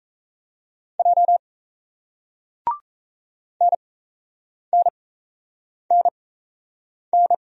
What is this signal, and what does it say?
Morse code, heard as a keyed tone, about 38 wpm, spaced as for 8 wpm, 700 Hz. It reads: J NNNN